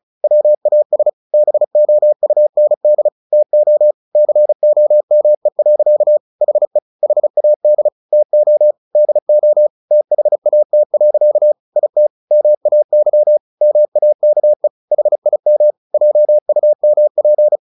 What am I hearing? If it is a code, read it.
Morse code, heard as a keyed tone, about 34 wpm, 600 Hz. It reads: WAS BOUND TO COME. HE HAD TO DO THAT. IT MAY MAKE HIM JUMP
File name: the old man and the sea 2 35wpm